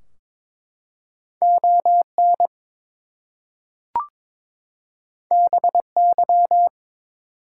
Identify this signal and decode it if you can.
Morse code, heard as a keyed tone, about 22 wpm, 700 Hz. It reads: ON BY
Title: Two Letter Words - 22wpm